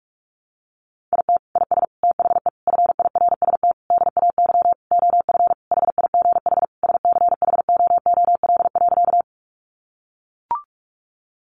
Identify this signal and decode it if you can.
Morse code, heard as a keyed tone, about 44 wpm, 700 Hz. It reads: IT IS THE FIRST DAY OF HIGH SCHOOL.
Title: Additional Easy Phrases 45wpm